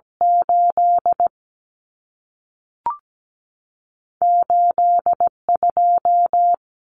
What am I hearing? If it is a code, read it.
Morse code, heard as a keyed tone, about 17 wpm, spaced as for 3 wpm, 700 Hz. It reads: 88 2
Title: Numbers - 2-Digit 17wpm